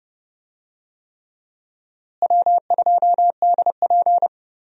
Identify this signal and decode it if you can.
Morse code, heard as a keyed tone, about 31 wpm, 700 Hz. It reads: W2DP
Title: USA Call Signs - 4 Characters - Repeated 3x - 30wpm